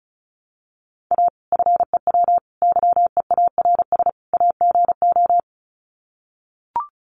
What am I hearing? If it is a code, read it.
Morse code, heard as a keyed tone, about 34 wpm, 700 Hz. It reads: A FEW YEARS AGO
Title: Sets of 4 Words 35wpm